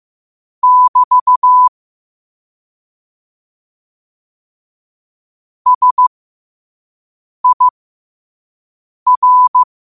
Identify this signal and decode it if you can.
Morse code, heard as a keyed tone, about 15 wpm, spaced as for 5 wpm, 1000 Hz. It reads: <BT> SIR
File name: qotdfiles / 5wpm